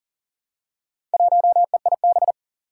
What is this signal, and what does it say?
Morse code, heard as a keyed tone, about 40 wpm, 700 Hz. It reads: 1EIB
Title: Four Challenging Letter-Number Combinations 40wpm